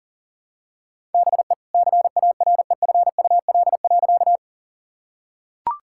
Morse code, heard as a keyed tone, about 40 wpm, 700 Hz. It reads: BE CAREFUL.
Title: Sentences from Top 1600 Words 40wpm